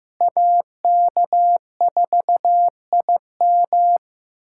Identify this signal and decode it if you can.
Morse code, heard as a keyed tone, about 15 wpm, 700 Hz. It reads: AK4IM
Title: USA Callsign DE Callsign - 5 Characters - 15wpm